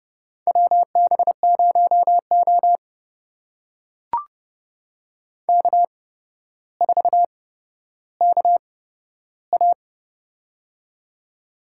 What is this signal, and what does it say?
Morse code, heard as a keyed tone, about 30 wpm, spaced as for 13 wpm, 700 Hz. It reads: WB0O K4KA
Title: USA Call Signs - 4 Characters - 8x Character Spacing - 30wpm